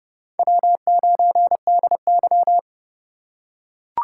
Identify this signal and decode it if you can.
Morse code, heard as a keyed tone, about 32 wpm, 700 Hz. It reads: W9DY
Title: K1USN SST QSO Practice - 30wpm